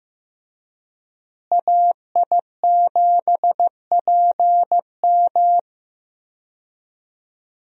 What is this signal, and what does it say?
Morse code, heard as a keyed tone, about 15 wpm, 700 Hz. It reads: AI7PM
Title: USA Call Signs - 5 Characters - Repeated 3x - 15wpm